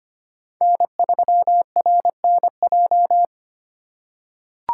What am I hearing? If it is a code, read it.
Morse code, heard as a keyed tone, about 25 wpm, 700 Hz. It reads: N3RNJ
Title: USA Call Signs - 5 Characters - 3x Character Spacing - 25wpm